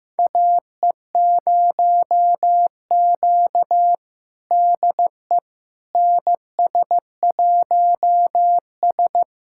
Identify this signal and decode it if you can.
Morse code, heard as a keyed tone, about 15 wpm, 700 Hz. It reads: AE0Q DE NS1S
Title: USA Callsign DE Callsign - 4 Characters - 15wpm